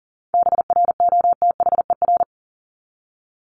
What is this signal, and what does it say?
Morse code, heard as a keyed tone, about 40 wpm, 700 Hz. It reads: BROTHER
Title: Top 1751-2000 Words 40wpm